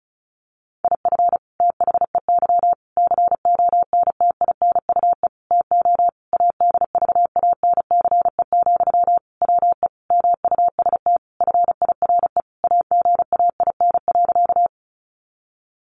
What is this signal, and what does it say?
Morse code, heard as a keyed tone, about 34 wpm, 700 Hz. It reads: IF THEY CONTINUE TO ADVANCE, WE MUST FIRE AGAIN.
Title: Sentences from Top 1400 Words 35wpm